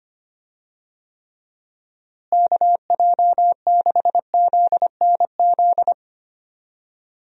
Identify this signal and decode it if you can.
Morse code, heard as a keyed tone, about 25 wpm, 700 Hz. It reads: KJ6ZNZ
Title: USA Call Signs - 6 Characters - Repeated 3x - 25wpm